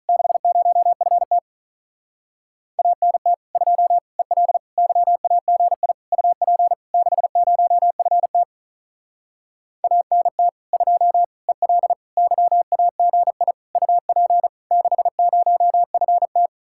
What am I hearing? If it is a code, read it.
Morse code, heard as a keyed tone, about 44 wpm, 700 Hz. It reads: 60FT ANT 2 EL YAGI UP 60FT ANT 2 EL YAGI UP 60FT